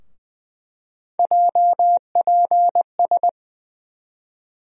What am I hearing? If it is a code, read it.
Morse code, heard as a keyed tone, about 20 wpm, 700 Hz. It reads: JPS
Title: USA Call Sign Postfixes - 3 Letters 20wpm